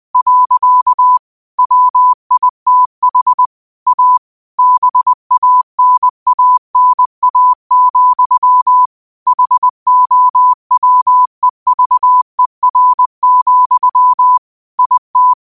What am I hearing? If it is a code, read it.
Morse code, heard as a keyed tone, about 20 wpm, 1000 Hz. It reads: . WITH A BANANA, HOWEVER, IT
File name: qotdfiles / 20wpm